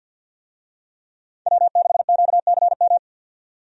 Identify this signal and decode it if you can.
Morse code, heard as a keyed tone, about 46 wpm, 700 Hz. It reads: W6QCM
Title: USA Call Signs - 5 Characters - 50wpm